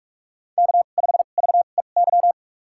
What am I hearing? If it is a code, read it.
Morse code, heard as a keyed tone, about 43 wpm, spaced as for 32 wpm, 700 Hz. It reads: K5VEY
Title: USA Call Signs - 5 Characters - 2x Character Spacing - 45wpm